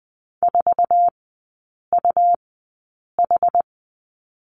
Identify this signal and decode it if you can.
Morse code, heard as a keyed tone, about 20 wpm, spaced as for 8 wpm, 700 Hz. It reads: 4UH
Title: 145 Lesson 24e - Letter y - Three Characters 20wpm